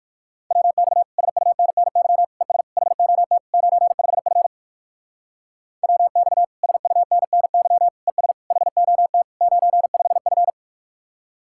Morse code, heard as a keyed tone, about 57 wpm, 700 Hz. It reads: WX SUNNY ES HOT 95F WX SUNNY ES HOT 95F